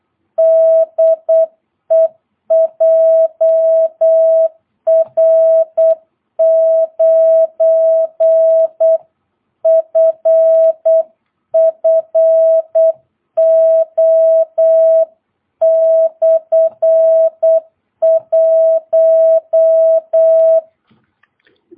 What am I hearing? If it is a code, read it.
Morse code, heard as a keyed tone, about 8 wpm, 660 Hz. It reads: DE JR9FFO/1
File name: jh8cbh / 2022cw